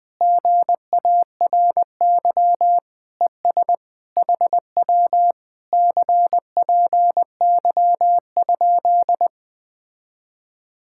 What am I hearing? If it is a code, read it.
Morse code, heard as a keyed tone, about 20 wpm, 700 Hz. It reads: GARY ES HW CPY?
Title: Speed-Racing Common QSO Phrases - 17wpm